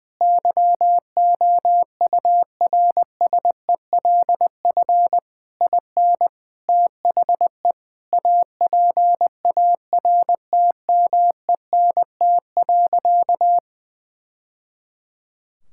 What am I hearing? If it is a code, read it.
Morse code, heard as a keyed tone, about 21 wpm, 700 Hz. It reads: YOURSELF IN THE APARTMENT.
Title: Sentences from Top 1000 Words - Review Part 6 - 20wpm